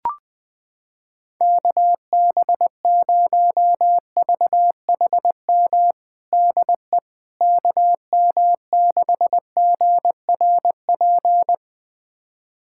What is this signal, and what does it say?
Morse code, heard as a keyed tone, about 20 wpm, 700 Hz. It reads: KB0VHM DE KM6GRP
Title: USA Callsign DE Callsign - 6 Characters - 20wpm